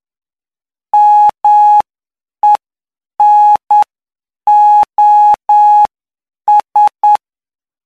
Morse code, heard as a keyed tone, about 9 wpm, spaced as for 7 wpm, 840 Hz. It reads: MENOS
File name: cw / py2lcd / 06-PY2LCD-Li��o 15